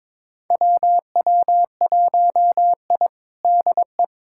Morse code, heard as a keyed tone, about 22 wpm, 700 Hz. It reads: WW1I DE
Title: USA Callsign DE Callsign - 4 Characters - 22wpm